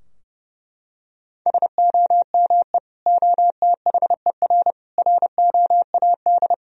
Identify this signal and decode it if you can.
Morse code, heard as a keyed tone, about 30 wpm, 700 Hz. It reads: SOME OTHER ROAD
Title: Sentences from Top 1000 Words - Review Part 4 - 30wpm